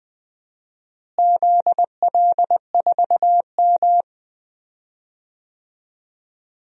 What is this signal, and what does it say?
Morse code, heard as a keyed tone, about 20 wpm, 700 Hz. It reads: ZL4M
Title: DXCC-100 Call Signs - 4 Characters - Repeated 2x - 20wpm